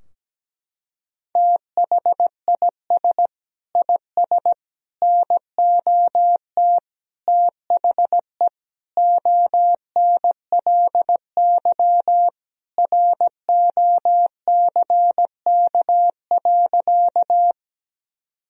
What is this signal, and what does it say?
Morse code, heard as a keyed tone, about 17 wpm, 700 Hz. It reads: THIS IS NOT THE ONLY ROCK.
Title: Sentences from Top 1000 Words - Review Part 2 - 17wpm